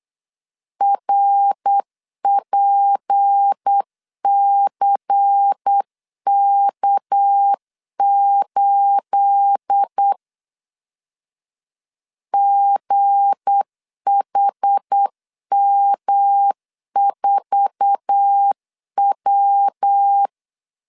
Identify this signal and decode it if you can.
Morse code, heard as a keyed tone, about 8 wpm, 800 Hz. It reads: RPCK8 GHM4W
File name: LEZIONI / 9 28 8wpm 3512